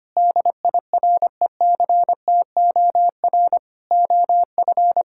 Morse code, heard as a keyed tone, about 25 wpm, 700 Hz. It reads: DIRECTOR OF
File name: practice / 25wpm / Sentences with 3-Word Phrases - 25wpm